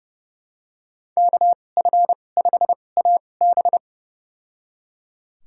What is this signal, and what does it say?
Morse code, heard as a keyed tone, about 31 wpm, spaced as for 22 wpm, 700 Hz. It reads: KF5AB
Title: USA Call Signs - 2x Character Spacing - 30wpm